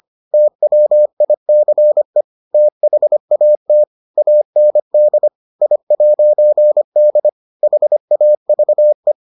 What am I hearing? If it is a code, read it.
Morse code, heard as a keyed tone, about 25 wpm, 600 Hz. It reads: TWICE THAT AND I'D HAVE